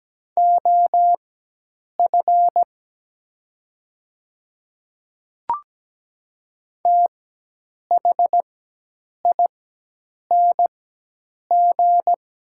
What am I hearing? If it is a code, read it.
Morse code, heard as a keyed tone, about 17 wpm, spaced as for 8 wpm, 700 Hz. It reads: OF THING
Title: 4x Character Spacing - Top 200 Words - 17wpm